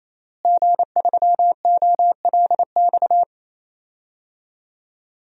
Morse code, heard as a keyed tone, about 27 wpm, 700 Hz. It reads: G3OLX
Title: DXCC-100 Call Signs - 5 Characters - Repeated 3x - 28wpm